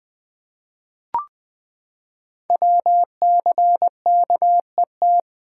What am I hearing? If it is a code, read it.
Morse code, heard as a keyed tone, about 20 wpm, 700 Hz. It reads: WCKET
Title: Code Groups - 5 Letters - 20wpm